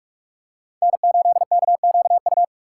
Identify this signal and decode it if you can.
Morse code, heard as a keyed tone, about 42 wpm, 700 Hz. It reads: N8KQU
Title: USA Call Signs - 5 Characters - 2x Character Spacing - 45wpm